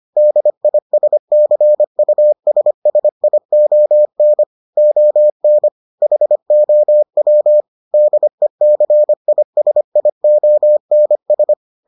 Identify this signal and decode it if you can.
Morse code, heard as a keyed tone, about 25 wpm, 600 Hz. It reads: DISCUSSION ON HOW DECISIONS